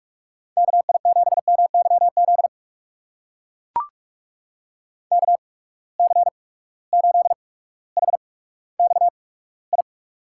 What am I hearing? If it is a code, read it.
Morse code, heard as a keyed tone, about 44 wpm, spaced as for 19 wpm, 700 Hz. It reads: KI7MYZ KC8HXI
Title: USA Call Signs - 6 Characters - 8x Character Spacing - 45wpm